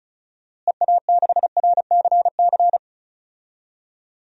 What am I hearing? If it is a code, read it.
Morse code, heard as a keyed tone, about 34 wpm, 700 Hz. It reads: EA6RCC
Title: DXCC-100 Call Signs - 6 Characters - Repeated 3x - 35wpm